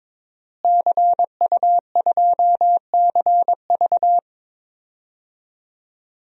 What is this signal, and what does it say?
Morse code, heard as a keyed tone, about 22 wpm, 700 Hz. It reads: CU2CV
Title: DXCC-100 Call Signs - 5 Characters - Repeated 3x - 22wpm